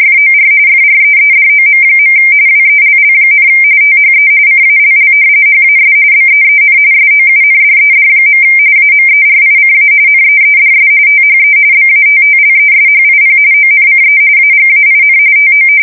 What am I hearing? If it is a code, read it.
RTTY, about 45.45 baud. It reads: FOR FREEDOM. WELL, THE CALL HAS BEEN SOUNDED. AMERICA NEEDS THE BATTLESHIP ONCE AGAIN TO
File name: fsk170 / battleshipnj